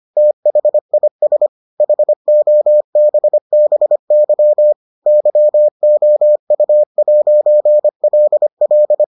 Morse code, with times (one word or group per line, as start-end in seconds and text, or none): 0.2-1.5 THIS
1.8-4.7 HOBBY
5.1-9.0 YOU'LL